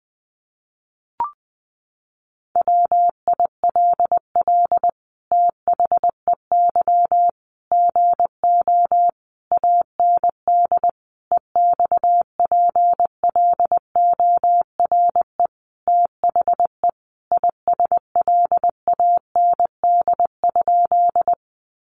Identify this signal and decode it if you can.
Morse code, heard as a keyed tone, about 20 wpm, 700 Hz. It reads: WILL THEY GO AND EXPLORE THE ISLAND?
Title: Sentences from Top 1100 Words 20wpm